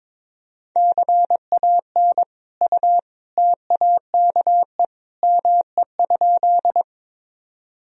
Morse code, heard as a keyed tone, about 22 wpm, 700 Hz. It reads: CAN U TAKE ME?
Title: Sentences from Top 200 Words and Abbreviations 22wpm